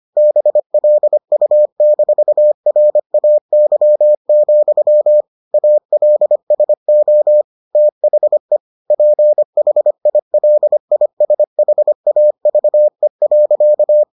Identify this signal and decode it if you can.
Morse code, heard as a keyed tone, about 25 wpm, 600 Hz. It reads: BLU-RAY, ALSO THE PHILISHAVE.